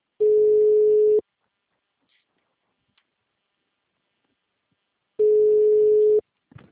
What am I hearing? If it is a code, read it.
Ringback tone (Europe, 425 Hz).